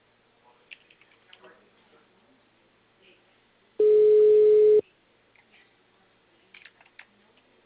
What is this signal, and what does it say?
Ringback tone (Europe, 425 Hz), with speech.